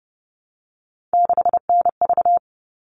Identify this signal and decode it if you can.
Morse code, heard as a keyed tone, about 30 wpm, 700 Hz. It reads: 6NV
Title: All Cut Numbers - 3-Digit 30wpm